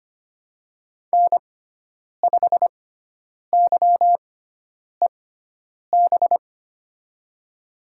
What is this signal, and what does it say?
Morse code, heard as a keyed tone, about 25 wpm, spaced as for 9 wpm, 700 Hz. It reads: N5YEB
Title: USA Call Signs - 5 Characters - 6x Character Spacing - 25wpm